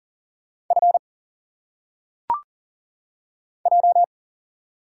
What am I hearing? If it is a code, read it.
Morse code, heard as a keyed tone, about 40 wpm, 700 Hz. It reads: F J
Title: Single Letters - Phonetic Pronunciation 40wpm